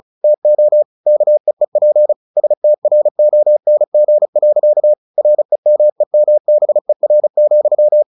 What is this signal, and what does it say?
Morse code, heard as a keyed tone, about 37 wpm, 600 Hz. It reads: TO KEEP STRONG. REMEMBER,
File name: the old man and the sea / the old man and the sea 1 35wpm